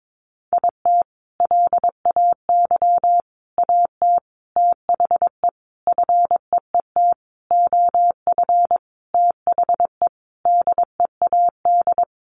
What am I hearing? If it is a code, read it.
Morse code, heard as a keyed tone, about 22 wpm, 700 Hz. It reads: IT LAY AT THE FEET OF THE DEAD
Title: Sentences from Top 1500 Words 22wpm